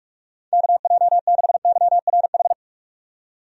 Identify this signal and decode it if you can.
Morse code, heard as a keyed tone, about 46 wpm, 700 Hz. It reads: KJ6YRH
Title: USA Call Signs - 6 Characters - 45wpm